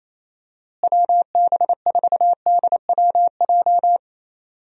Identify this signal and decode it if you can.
Morse code, heard as a keyed tone, about 27 wpm, 700 Hz. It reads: WB4DWJ
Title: USA Call Signs - 6 Characters - Repeated 2x - 28wpm